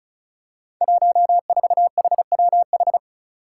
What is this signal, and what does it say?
Morse code, heard as a keyed tone, about 34 wpm, 700 Hz. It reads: 14HWH
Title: Five Challenging Letter-Number Combinations 35wpm